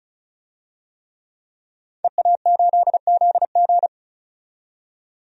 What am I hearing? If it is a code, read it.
Morse code, heard as a keyed tone, about 34 wpm, 700 Hz. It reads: EA8ZG
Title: DXCC-100 Call Signs - 5 Characters - 35wpm